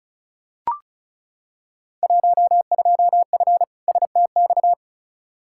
Morse code, heard as a keyed tone, about 36 wpm, 700 Hz. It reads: 12F STX
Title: Field Day Exchanges Encore - 35wpm